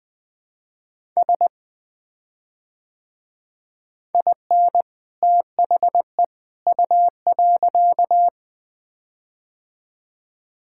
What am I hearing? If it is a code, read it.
Morse code, heard as a keyed tone, about 20 wpm, 700 Hz. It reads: S IN THE U.
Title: Build A Phrase - 3-Word Phrases - 20wpm